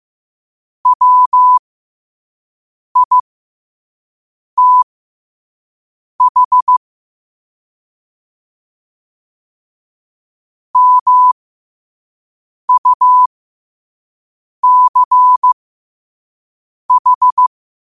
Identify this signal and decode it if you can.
Morse code, heard as a keyed tone, about 15 wpm, spaced as for 5 wpm, 1000 Hz. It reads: WITH MUCH
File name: qotdfiles / 5wpm